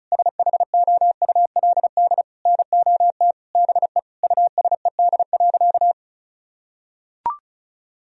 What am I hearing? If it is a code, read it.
Morse code, heard as a keyed tone, about 34 wpm, 700 Hz. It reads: SHOULD NOT BE USED.
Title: Sentences from Top 1000 Words - Review Part 4 - 35wpm